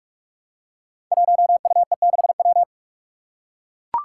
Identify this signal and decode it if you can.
Morse code, heard as a keyed tone, about 42 wpm, 700 Hz. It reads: 1UE6W
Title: Five Challenging Letter-Number Combinations 45wpm